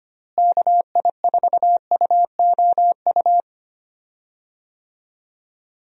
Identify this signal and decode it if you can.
Morse code, heard as a keyed tone, about 25 wpm, 700 Hz. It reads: KI4UOU
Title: USA Call Signs - 6 Characters - Repeated 3x - 25wpm